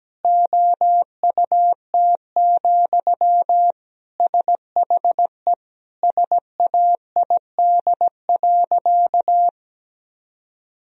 Morse code, heard as a keyed tone, about 17 wpm, 700 Hz. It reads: OUT, SHE SAID.
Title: Sentences from Top 1000 Words - Review Part 3 - 17wpm